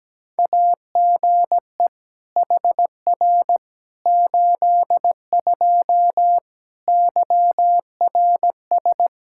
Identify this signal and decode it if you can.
Morse code, heard as a keyed tone, about 17 wpm, 700 Hz. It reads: AGE HR 82 YRS
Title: K1USN SST QSO Practice 17wpm